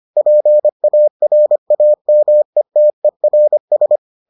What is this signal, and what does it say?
Morse code, heard as a keyed tone, about 25 wpm, 600 Hz. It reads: PARAMETERS